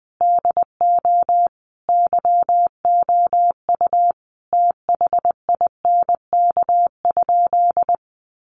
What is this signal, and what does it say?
Morse code, heard as a keyed tone, about 20 wpm, 700 Hz. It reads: DO YOU THINK?
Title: Sentences from Top 200 Words 20wpm